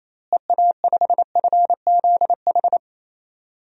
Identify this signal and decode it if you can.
Morse code, heard as a keyed tone, about 27 wpm, 700 Hz. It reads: EA5FZH
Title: DXCC-100 Call Signs - 6 Characters - Repeated 2x - 28wpm